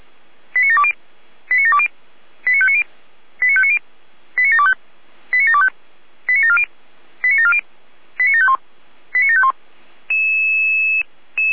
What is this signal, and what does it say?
Five-tone selective call (ZVEI), heard as quick runs of five tones, tones of about 70 ms. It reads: 89720 89720 89590 89590 89725 89725 89740 89740 89751 89751